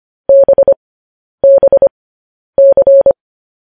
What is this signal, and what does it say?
Morse code, heard as a keyed tone, about 25 wpm, spaced as for 10 wpm, 550 Hz. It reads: BBC
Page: news headlines in morse code at 10 wpm